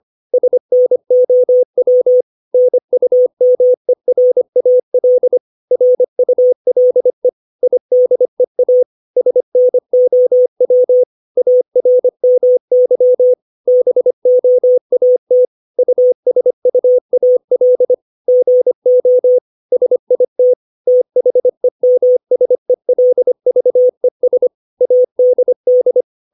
Morse code, heard as a keyed tone, about 25 wpm, 500 Hz. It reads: SNOW NUMERAL RULE IDEA SNOW ARMY BOAT USUAL GO SIT THEMSELVES ADD
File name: 632 words / 632 Words 25WPM